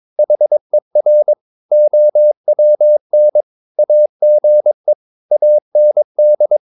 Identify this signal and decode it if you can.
Morse code, heard as a keyed tone, about 22 wpm, 600 Hz. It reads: HER OWN AGE AND
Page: Romanen Never af Ken Follet på engelsk.